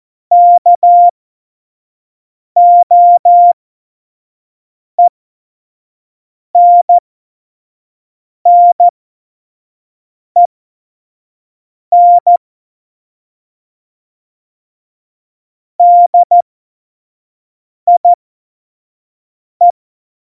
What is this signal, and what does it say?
Morse code, heard as a keyed tone, about 14 wpm, spaced as for 5 wpm, 700 Hz. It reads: KOENNEN DIE
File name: Morsekurs / Witze / Witz 23. Januar 2018, 25 F70